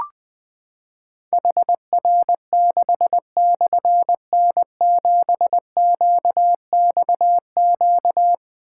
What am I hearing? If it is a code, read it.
Morse code, heard as a keyed tone, about 20 wpm, 700 Hz. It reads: HR6/N7QXQ
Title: Difficult Call Signs 20wpm